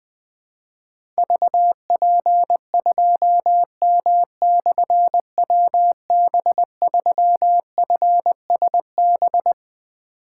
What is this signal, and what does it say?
Morse code, heard as a keyed tone, about 20 wpm, 700 Hz. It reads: VP2M/WB3FSB
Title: Difficult Call Signs 20wpm